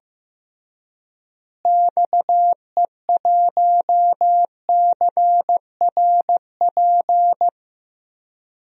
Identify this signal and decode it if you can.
Morse code, heard as a keyed tone, about 15 wpm, 700 Hz. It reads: XE1CRP
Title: DXCC-100 Call Signs - 6 Characters - Repeated 3x - 15wpm